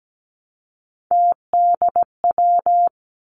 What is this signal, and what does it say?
Morse code, heard as a keyed tone, about 17 wpm, 700 Hz. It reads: TDW
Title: Three Challenging Letter-Number Combinations 17wpm